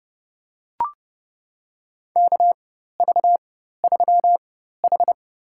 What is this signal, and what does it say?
Morse code, heard as a keyed tone, about 29 wpm, spaced as for 14 wpm, 700 Hz. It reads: KV3H
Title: USA Call Signs - 4 Characters - 4x Character Spacing - 30wpm